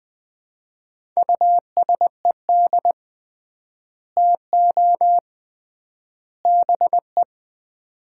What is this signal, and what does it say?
Morse code, heard as a keyed tone, about 20 wpm, 700 Hz. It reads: USED TO BE
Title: 3x Word-Spacing - Sets of 3 Words - 20wpm